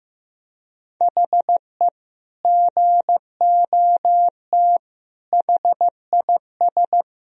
Morse code, heard as a keyed tone, about 14 wpm, 700 Hz. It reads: HE GOT HIS
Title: Sentences from Top 1000 Words - Review Part 1 - 15wpm